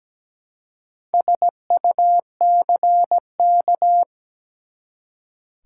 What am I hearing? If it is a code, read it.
Morse code, heard as a keyed tone, about 17 wpm, 700 Hz. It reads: SUCK
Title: Four Letter Words - 17wpm